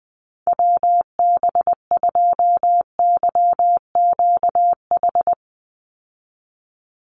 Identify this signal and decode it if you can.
Morse code, heard as a keyed tone, about 19 wpm, 700 Hz. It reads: WB2YQH
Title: Call Signs Repeated 2x - 20wpm